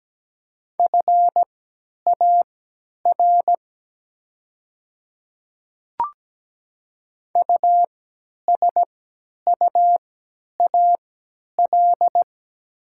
Morse code, heard as a keyed tone, about 17 wpm, spaced as for 10 wpm, 700 Hz. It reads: FAR USUAL
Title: 3x Character Spacing - Top 500 Words - 17wpm